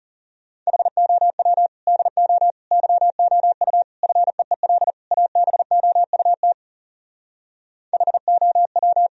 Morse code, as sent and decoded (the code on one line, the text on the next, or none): .... --- .-- / -.. --- / -.-- --- ..- / ..-. . .-.. / .- -... --- ..- - / .... --- .--
HOW DO YOU FEEL ABOUT HOW